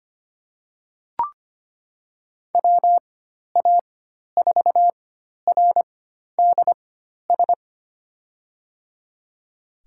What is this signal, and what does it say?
Morse code, heard as a keyed tone, about 25 wpm, spaced as for 12 wpm, 700 Hz. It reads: WA4RDS
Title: USA Call Signs - 6 Characters - 4x Character Spacing - 25wpm